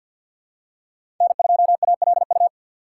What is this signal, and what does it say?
Morse code, heard as a keyed tone, about 51 wpm, 700 Hz. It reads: N2ALU
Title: USA Call Signs - 2x Character Spacing - 50wpm